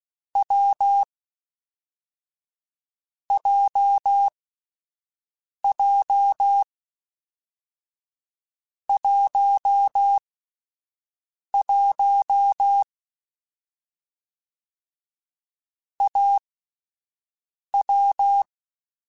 Morse code, heard as a keyed tone, about 16 wpm, 800 Hz. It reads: W J J 1 1 A W